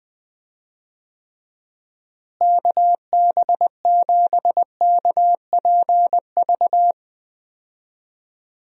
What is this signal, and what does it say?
Morse code, heard as a keyed tone, about 20 wpm, 700 Hz. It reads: KB7KPV
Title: USA Call Signs - 6 Characters - Repeated 3x - 20wpm